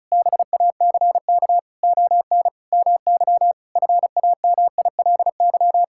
Morse code, heard as a keyed tone, about 36 wpm, 700 Hz. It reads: BACK ON MY FAMILY
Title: Sentences from Top 1000 Words - Review Part 6 - 35wpm